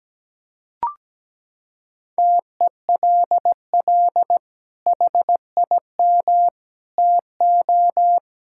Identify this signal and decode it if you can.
Morse code, heard as a keyed tone, about 17 wpm, 700 Hz. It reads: TELL HIM TO
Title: Sentences from Top 1000 Words - Review Part 6 - 17wpm